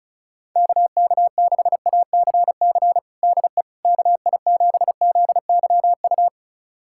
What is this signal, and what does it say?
Morse code, heard as a keyed tone, about 34 wpm, 700 Hz. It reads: KK6ACC DE KI7ZYU
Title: USA Callsign DE Callsign - 6 Characters - 35wpm